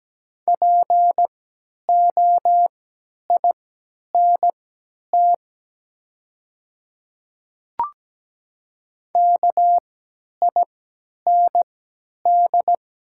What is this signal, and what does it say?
Morse code, heard as a keyed tone, about 17 wpm, spaced as for 10 wpm, 700 Hz. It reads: POINT KIND
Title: 3x Character Spacing - Top 300 Words - 17wpm